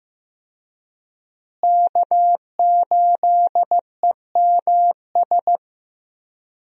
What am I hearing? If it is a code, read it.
Morse code, heard as a keyed tone, about 15 wpm, 700 Hz. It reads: K8EMS
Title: USA Call Signs - 5 Characters - Repeated 3x - 15wpm